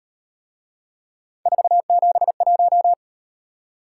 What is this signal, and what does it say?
Morse code, heard as a keyed tone, about 38 wpm, 700 Hz. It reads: 471